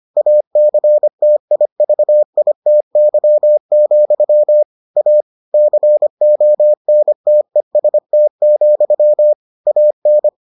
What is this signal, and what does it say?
Morse code, heard as a keyed tone, about 25 wpm, 600 Hz. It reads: ACTIVITY, A CONTEST, AN